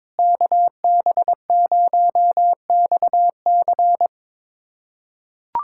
Morse code, heard as a keyed tone, about 22 wpm, 700 Hz. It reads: KB0XC
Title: USA Callsign DE Callsign - 5 Characters - 22wpm